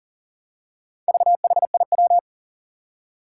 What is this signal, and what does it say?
Morse code, heard as a keyed tone, about 37 wpm, 700 Hz. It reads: VHIW
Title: Four Challenging Letter-Number Combinations 40wpm